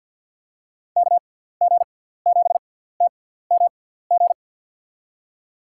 Morse code, heard as a keyed tone, about 52 wpm, spaced as for 17 wpm, 700 Hz. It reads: KG7TMG
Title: USA Call Signs - 6 Characters - 6x Character Spacing - 50wpm